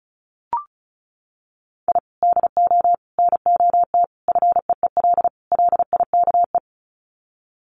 Morse code, heard as a keyed tone, about 34 wpm, 700 Hz. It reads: I DO NOT FEEL LIKE
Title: Sets of 5 Words Encore 35wpm